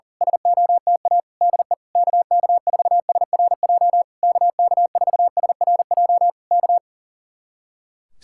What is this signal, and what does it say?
Morse code, heard as a keyed tone, about 37 wpm, 700 Hz. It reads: SOTA DE KK4SRJ KK4SRJ K
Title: Speed-Racing Common QSO Phrases - 40wpm